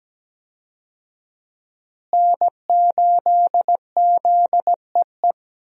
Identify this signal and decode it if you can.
Morse code, heard as a keyed tone, about 17 wpm, 700 Hz. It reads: N8ZEE
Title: USA Call Signs - 5 Characters - Repeated 3x - 17wpm